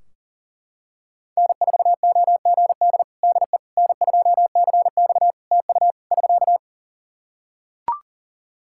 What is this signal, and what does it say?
Morse code, heard as a keyed tone, about 40 wpm, 700 Hz. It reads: N4OGD DE N2CX TU <SK>
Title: Practice QSO - Exchange 4 - Ending - 40wpm